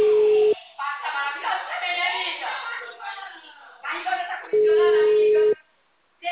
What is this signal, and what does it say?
Ringback tone (Europe, 425 Hz), with speech.